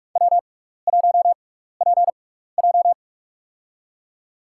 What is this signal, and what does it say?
Morse code, heard as a keyed tone, about 44 wpm, spaced as for 15 wpm, 700 Hz. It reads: W1PJ
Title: USA Call Signs - 4 Characters - 6x Character Spacing - 45wpm